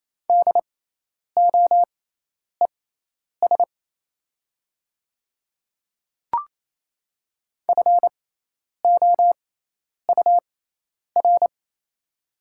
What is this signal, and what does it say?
Morse code, heard as a keyed tone, about 28 wpm, spaced as for 10 wpm, 700 Hz. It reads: DOES FOUR
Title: 6x Character Spacing - Top 300 Words - 28wpm